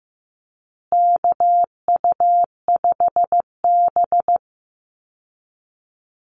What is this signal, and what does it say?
Morse code, heard as a keyed tone, about 15 wpm, 700 Hz. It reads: KU5B
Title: Call Signs Repeated 2x - 15wpm